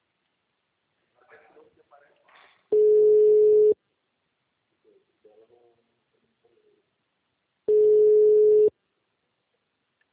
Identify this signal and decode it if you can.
Ringback tone (Europe, 425 Hz), with speech.